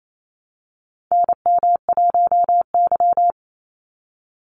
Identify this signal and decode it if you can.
Morse code, heard as a keyed tone, about 28 wpm, 700 Hz. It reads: NM1Y
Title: Call Signs - 28wpm